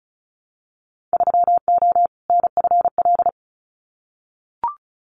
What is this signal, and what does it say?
Morse code, heard as a keyed tone, about 34 wpm, 700 Hz. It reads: 3O NFL
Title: Winter Field Day Exchanges 35wpm